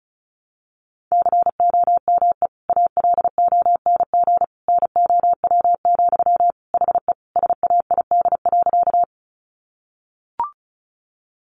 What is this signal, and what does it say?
Morse code, heard as a keyed tone, about 34 wpm, 700 Hz. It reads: COME ALONG NOW, HE SAID.
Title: Sentences from Top 700 Words 35wpm